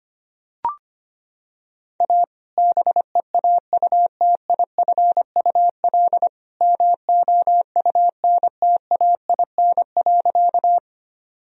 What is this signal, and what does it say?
Morse code, heard as a keyed tone, about 25 wpm, 700 Hz. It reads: A BEAUTIFUL MOUNTAIN.
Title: Sentences from Top 1000 Words - Review Part 1 - 25wpm